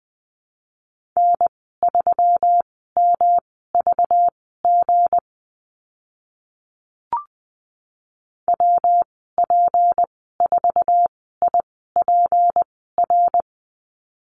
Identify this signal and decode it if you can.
Morse code, heard as a keyed tone, about 20 wpm, spaced as for 14 wpm, 700 Hz. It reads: N3MVG WP4IPR
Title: 219 Lesson 35g - Number 0 - Call Signs 20wpm